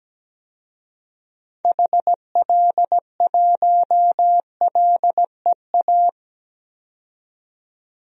Morse code, heard as a keyed tone, about 17 wpm, 700 Hz. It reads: HL1LEA
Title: DXCC-100 Call Signs - 6 Characters - 17wpm